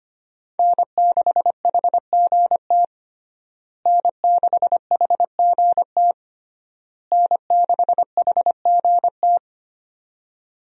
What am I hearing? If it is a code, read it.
Morse code, heard as a keyed tone, about 25 wpm, 700 Hz. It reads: N6HGT N6HGT N6HGT
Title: USA Call Signs - 5 Characters - Repeated 3x - 25wpm